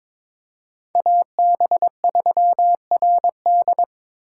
Morse code, heard as a keyed tone, about 26 wpm, 700 Hz. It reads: AB3RD
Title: Speed Racing - USA 2x2 Call Signs 22wpm